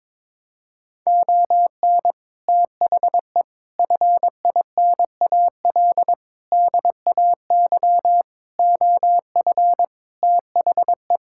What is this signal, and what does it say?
Morse code, heard as a keyed tone, about 22 wpm, 700 Hz. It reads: ON THE FINAL DAY OF THE
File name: practice / 22wpm / Sets of 7 Words - 22wpm